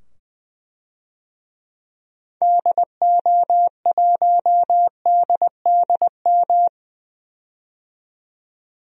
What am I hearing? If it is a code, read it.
Morse code, heard as a keyed tone, about 20 wpm, 700 Hz. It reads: DO1DDM
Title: DXCC-100 Call Signs - 6 Characters - 20wpm